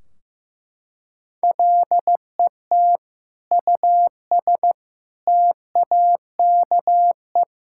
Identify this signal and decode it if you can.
Morse code, heard as a keyed tone, about 14 wpm, 700 Hz. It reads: LET US TAKE
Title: Sentences from Top 1600 Words 15wpm